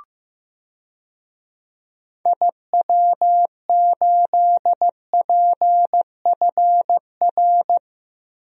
Morse code, heard as a keyed tone, about 15 wpm, 700 Hz. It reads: IW8PFR
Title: DXCC-100 Call Signs - 6 Characters - Repeated 3x - 15wpm